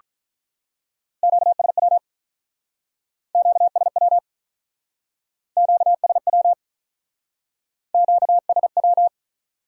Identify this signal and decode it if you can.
Morse code, heard as a keyed tone, about 43 wpm, 700 Hz. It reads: QSW QSW QSW QSW
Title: Speed-Racing Q-Codes - 35wpm